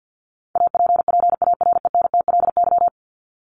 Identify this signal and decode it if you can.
Morse code, heard as a keyed tone, about 50 wpm, 700 Hz. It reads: APPARENTLY
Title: LY Suffix - Top 100 Words with Base Words 50wpm